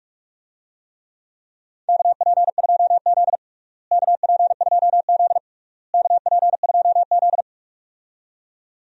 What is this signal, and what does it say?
Morse code, heard as a keyed tone, about 46 wpm, 700 Hz. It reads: KP2Z KP2Z KP2Z
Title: DXCC-100 Call Signs - 4 Characters - Repeated 3x - 45wpm